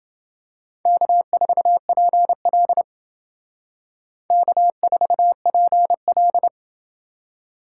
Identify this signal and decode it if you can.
Morse code, heard as a keyed tone, about 28 wpm, 700 Hz. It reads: K4PL K4PL